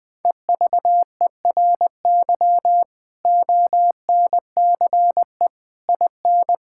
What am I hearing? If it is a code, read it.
Morse code, heard as a keyed tone, about 20 wpm, 700 Hz. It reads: EVERY ONCE IN
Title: Build A Phrase - 5-Word Phrases - 20wpm